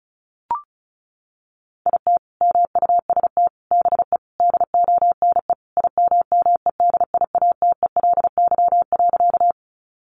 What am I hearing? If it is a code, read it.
Morse code, heard as a keyed tone, about 34 wpm, 700 Hz. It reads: IT MUST BE DONE IMMEDIATELY.
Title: Sentences from Top 1100 Words 35wpm